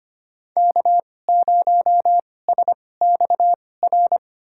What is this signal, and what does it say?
Morse code, heard as a keyed tone, about 25 wpm, spaced as for 18 wpm, 700 Hz. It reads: K0SXR
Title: USA Call Signs - 5 Characters - 2x Character Spacing - 25wpm